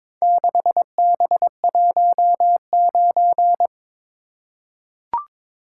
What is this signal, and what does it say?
Morse code, heard as a keyed tone, about 22 wpm, 700 Hz. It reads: 6B19
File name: practice / 22wpm / Four Letter-Number Combinations - 22wpm